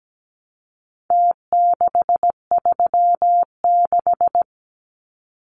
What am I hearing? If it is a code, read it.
Morse code, heard as a keyed tone, about 17 wpm, 700 Hz. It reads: T636
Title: Numbers - Contest Style - 4-Digit 17wpm